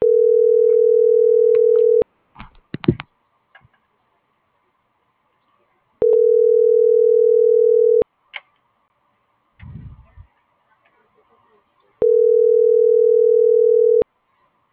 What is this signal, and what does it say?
Ringback tone (North America), 440 + 480 Hz.